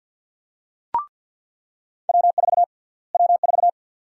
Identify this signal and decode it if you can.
Morse code, heard as a keyed tone, about 49 wpm, 700 Hz. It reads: W4 W4
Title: USA Call Sign Prefixes - Repeated 2x - 50wpm